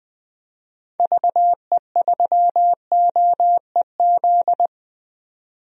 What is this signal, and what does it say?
Morse code, heard as a keyed tone, about 20 wpm, 700 Hz. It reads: VE3OEZ
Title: DXCC-100 Call Signs - 6 Characters - 20wpm